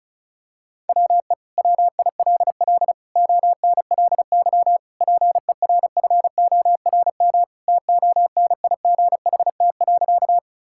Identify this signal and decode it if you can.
Morse code, heard as a keyed tone, about 34 wpm, 700 Hz. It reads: WE WILL ONLY PERFORM TONIGHT.
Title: Sentences from Top 1000 Words - Review Part 5 - 35wpm